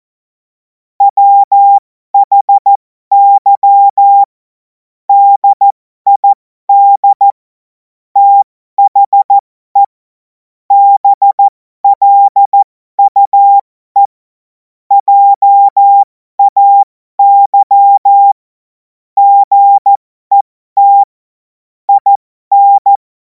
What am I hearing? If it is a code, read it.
Morse code, heard as a keyed tone, about 14 wpm, spaced as for 12 wpm, 800 Hz. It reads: WHY DID THE BLUE JAY GET IN
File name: Morsekurs / Witze